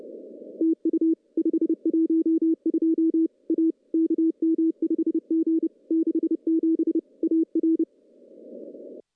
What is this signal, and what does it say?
Morse code, heard as a keyed tone, about 30 wpm, 320 Hz. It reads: TU 512 A KM5G 67 AR
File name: contests / CW-SS-k5tr- -1400